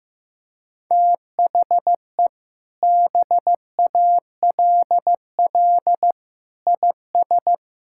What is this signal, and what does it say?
Morse code, heard as a keyed tone, about 15 wpm, 700 Hz. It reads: THE BALL IS
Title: Sentences from Top 1000 Words - Review Part 6 - 15wpm